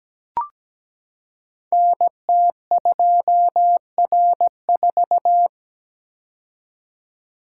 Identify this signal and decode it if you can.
Morse code, heard as a keyed tone, about 17 wpm, 700 Hz. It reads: NT2R4
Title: Code Groups - 5 Letter:Numbers - 17wpm